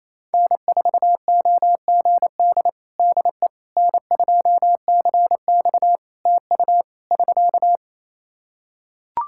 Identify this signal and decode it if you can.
Morse code, heard as a keyed tone, about 28 wpm, 700 Hz. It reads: N4OGD DE N2CX TU <SK>
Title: Practice QSO - Exchange 4 - Ending - 28wpm